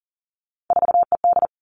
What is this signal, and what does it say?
Morse code, heard as a keyed tone, about 42 wpm, 700 Hz. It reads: 4ED
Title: Three Challenging Letter-Number Combinations 40wpm